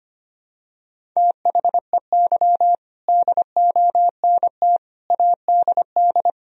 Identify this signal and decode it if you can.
Morse code, heard as a keyed tone, about 25 wpm, 700 Hz. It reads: THEY DONT ADD
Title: Sentences from Top 1000 Words - Review Part 1 - 25wpm